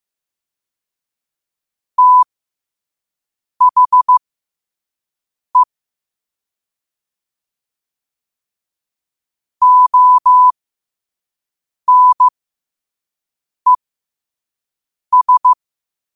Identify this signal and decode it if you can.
Morse code, heard as a keyed tone, about 15 wpm, spaced as for 5 wpm, 1000 Hz. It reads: THE ONES
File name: qotdfiles / 5wpm